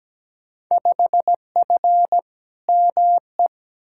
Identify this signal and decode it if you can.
Morse code, heard as a keyed tone, about 17 wpm, 700 Hz. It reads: 5F ME
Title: Field Day Exchanges Encore - 17wpm